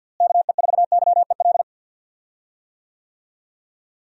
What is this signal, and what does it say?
Morse code, heard as a keyed tone, about 46 wpm, 700 Hz. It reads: KE4YEL
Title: USA Call Signs - 6 Characters - Repeated 3x - 50wpm